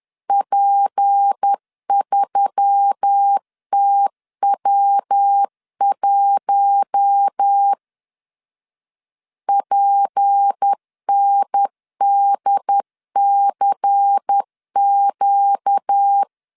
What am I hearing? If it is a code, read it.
Morse code, heard as a keyed tone, about 10 wpm, 800 Hz. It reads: P3TW1 PNDCQ
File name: LEZIONI / 6 PKX0 / 6 PKX 10wpm 2426